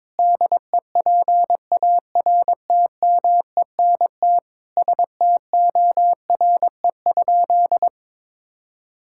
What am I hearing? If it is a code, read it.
Morse code, heard as a keyed tone, about 22 wpm, 700 Hz. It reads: DEPARTMENT STORE?
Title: Sentences from Top 1000 Words - Review Part 5 - 22wpm